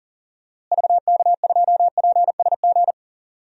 Morse code, heard as a keyed tone, about 40 wpm, 700 Hz. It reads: VK2PSG
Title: DXCC-100 Call Signs - 6 Characters - Repeated 2x - 40wpm